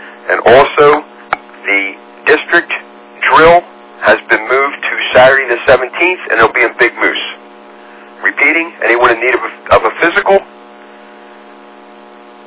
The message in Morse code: none

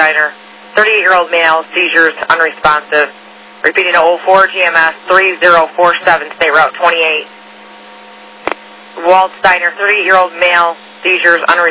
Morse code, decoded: none

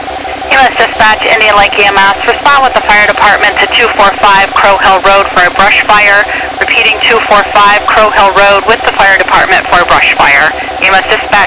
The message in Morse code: none

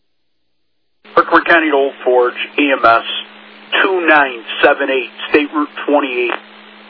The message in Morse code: none

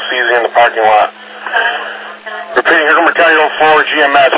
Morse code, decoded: none

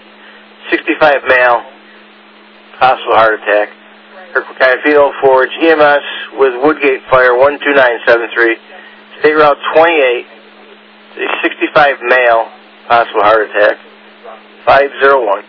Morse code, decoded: none

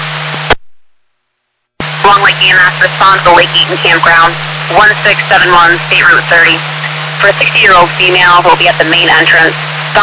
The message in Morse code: none

